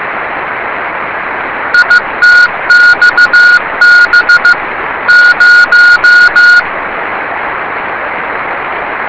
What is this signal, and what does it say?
Morse code, heard as a keyed tone, about 15 wpm, 1440 Hz, with radio noise. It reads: ITXB 0
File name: rs30 cw